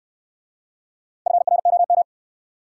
Morse code, heard as a keyed tone, about 63 wpm, 700 Hz. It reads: FROM